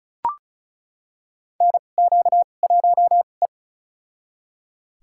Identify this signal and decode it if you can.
Morse code, heard as a keyed tone, about 36 wpm, spaced as for 25 wpm, 690 Hz. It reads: NQ1E
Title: USA Call Signs - 4 Characters - 2x Character Spacing - 35wpm